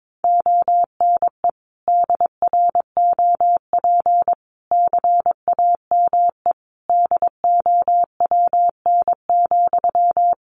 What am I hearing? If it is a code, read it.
Morse code, heard as a keyed tone, about 22 wpm, 700 Hz. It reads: ONE DROP CAME DOWN,
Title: Sentences from Top 1400 Words 22wpm